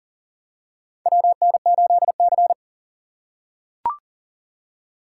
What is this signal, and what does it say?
Morse code, heard as a keyed tone, about 37 wpm, 700 Hz. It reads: WN8C
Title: USA Call Signs - 4 Characters - 3x Character Spacing - 40wpm